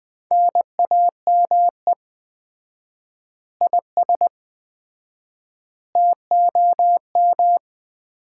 Morse code, heard as a keyed tone, about 20 wpm, 700 Hz. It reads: NAME IS TOM
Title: Common QSO Phrases - 4x Word-Spacing - 20wpm